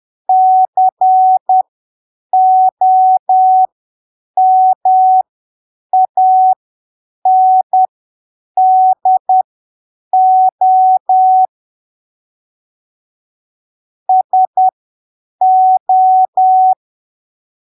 Morse code, heard as a keyed tone, about 10 wpm, spaced as for 7 wpm, 750 Hz. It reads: COMANDO SO
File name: arquivos / texto / texto 13 10ppm